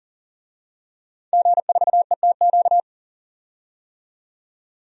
Morse code, heard as a keyed tone, about 40 wpm, 700 Hz. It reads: G4ETQ
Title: DXCC-100 Call Signs - 5 Characters - 40wpm